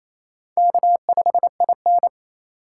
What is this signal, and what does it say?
Morse code, heard as a keyed tone, about 34 wpm, 700 Hz. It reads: K5IN
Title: Speed Racing - USA 1x2 Call Signs 28wpm